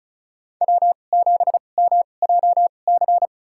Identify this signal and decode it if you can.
Morse code, heard as a keyed tone, about 34 wpm, spaced as for 25 wpm, 700 Hz. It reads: W7MJC
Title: USA Call Signs - 2x Character Spacing - 35wpm